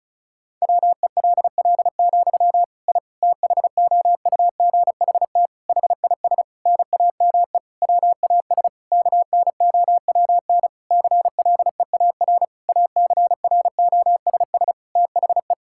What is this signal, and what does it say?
Morse code, heard as a keyed tone, about 34 wpm, 700 Hz. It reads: WELL, I THOUGHT HIS NAME WAS KNOWN CLEAR ACROSS THE
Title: Sentences from Top 1000 Words - Review Part 5 - 35wpm